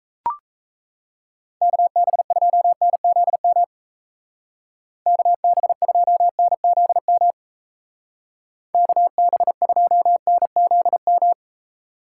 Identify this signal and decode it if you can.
Morse code, heard as a keyed tone, about 35 wpm, 700 Hz. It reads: KB2NZM KB2NZM KB2NZM